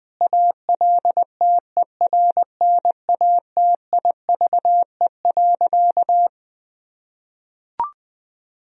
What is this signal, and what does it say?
Morse code, heard as a keyed tone, about 20 wpm, 700 Hz. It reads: ALTERNATIVE.
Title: Sentences from Top 1000 Words - Review Part 4 - 20wpm